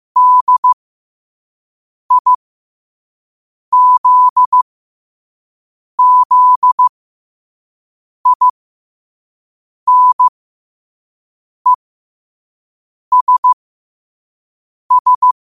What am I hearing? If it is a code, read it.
Morse code, heard as a keyed tone, about 15 wpm, spaced as for 5 wpm, 1000 Hz. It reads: DIZZINESS
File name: qotdfiles / 5wpm